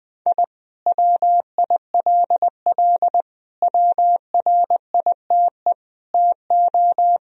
Morse code, heard as a keyed tone, about 21 wpm, 700 Hz. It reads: I WILL WRITE TO
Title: Sentences from Top 1000 Words - Review Part 1 - 20wpm